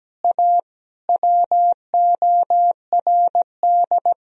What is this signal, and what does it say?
Morse code, heard as a keyed tone, about 17 wpm, 700 Hz. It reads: A WORD
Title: Sentences from Top 300 Words and Abbreviations 17wpm